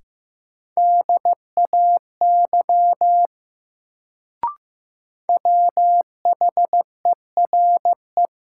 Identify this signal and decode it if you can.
Morse code, heard as a keyed tone, about 15 wpm, 700 Hz. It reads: DAY WHERE
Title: Sentences from Top 200 Words - New Word Drill - 15wpm